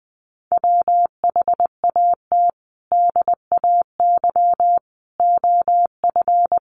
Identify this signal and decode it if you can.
Morse code, heard as a keyed tone, about 20 wpm, 700 Hz. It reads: WHAT DAY OF
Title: Sentences from Top 800 Words 20wpm